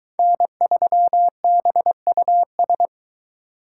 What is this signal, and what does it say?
Morse code, heard as a keyed tone, about 22 wpm, 700 Hz. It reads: N3BUS